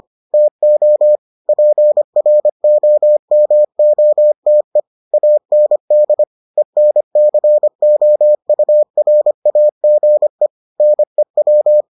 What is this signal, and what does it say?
Morse code, heard as a keyed tone, about 25 wpm, 600 Hz. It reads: TO PROMOTE AND ENCOURAGE NEW